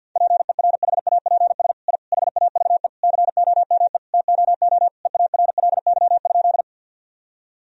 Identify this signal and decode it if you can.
Morse code, heard as a keyed tone, about 50 wpm, 700 Hz. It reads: PERHAPS I HAVE COME TOO EARLY?
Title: Sentences from Top 1000 Words - Review Part 2 - 50wpm